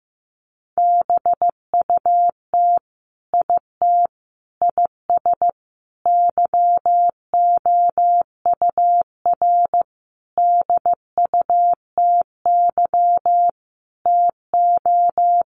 Morse code, heard as a keyed tone, about 15 wpm, 700 Hz. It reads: BUT IT IS YOUR DUTY TO
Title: Sentences from Top 1500 Words 15wpm